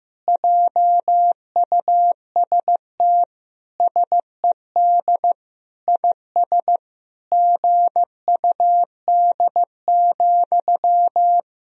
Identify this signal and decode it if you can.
Morse code, heard as a keyed tone, about 15 wpm, 700 Hz. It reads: JUST SED IS GUD,
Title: Sentences from Top 200 Words and Abbreviations 15wpm